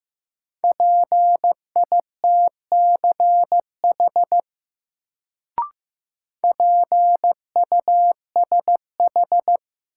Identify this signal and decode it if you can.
Morse code, heard as a keyed tone, about 15 wpm, 700 Hz. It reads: PITCH PUSH
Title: Sentences from Top 1300 Words - New Word Drill - 15wpm